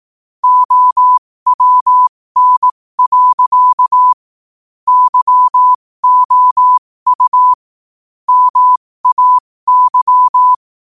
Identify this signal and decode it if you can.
Morse code, heard as a keyed tone, about 18 wpm, spaced as for 15 wpm, 1000 Hz. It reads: OWN. YOU MAY
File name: qotdfiles / 15wpm